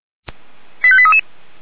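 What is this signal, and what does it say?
Five-tone selective call (ZVEI), heard as quick runs of five tones, tones of about 70 ms. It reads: 85833
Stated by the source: ZVEI five-tone selective call received by SDR, fire brigade feuerwehr-orlamuende.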